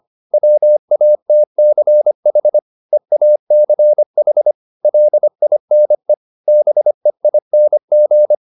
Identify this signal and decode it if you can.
Morse code, heard as a keyed tone, about 25 wpm, 600 Hz. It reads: WATCH EACH LINE BEING